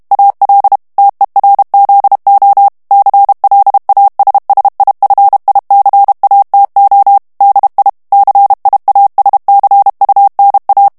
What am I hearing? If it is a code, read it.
Morse code, heard as a keyed tone, about 32 wpm, 800 Hz. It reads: AL TERZO CLASSIFICATO DI CIASCUNA
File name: ITALIEN texte clair 2 vitesse 30 mots minute